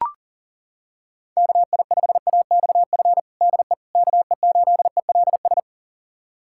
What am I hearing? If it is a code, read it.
Morse code, heard as a keyed tone, about 40 wpm, 700 Hz. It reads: KI5AXF DE KE8ELS
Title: USA Callsign DE Callsign - 6 Characters - 40wpm